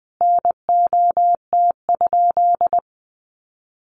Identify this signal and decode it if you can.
Morse code, heard as a keyed tone, about 20 wpm, 700 Hz. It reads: NOT?
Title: Sentences from Top 600 Words 20wpm